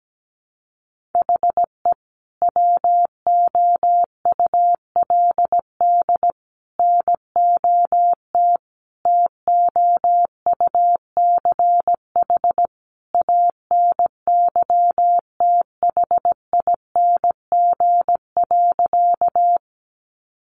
Morse code, heard as a keyed tone, about 17 wpm, 700 Hz. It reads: HE WOULD NOT TOUCH ANYTHING.
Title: Sentences from Top 1200 Words 17wpm